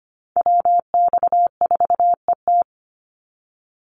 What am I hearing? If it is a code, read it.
Morse code, heard as a keyed tone, about 25 wpm, 700 Hz. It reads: WX4ET
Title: NA SOTA Chaser Callsigns 25wpm